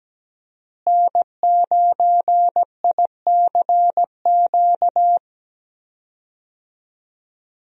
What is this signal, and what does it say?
Morse code, heard as a keyed tone, about 17 wpm, 700 Hz. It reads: N9ICQ
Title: USA Call Signs - 5 Characters - Repeated 2x - 17wpm